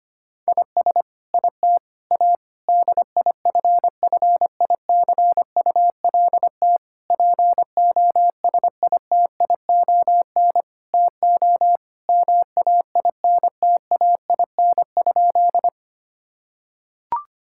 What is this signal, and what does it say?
Morse code, heard as a keyed tone, about 25 wpm, 700 Hz. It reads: IS IT A DIFFICULT POSITION TO MAINTAIN?
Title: Sentences from Top 1000 Words - Review Part 5 - 25wpm